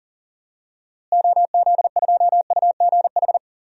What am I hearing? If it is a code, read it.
Morse code, heard as a keyed tone, about 40 wpm, 700 Hz. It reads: OZ2UGH
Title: DXCC-100 Call Signs - 6 Characters - Repeated 3x - 40wpm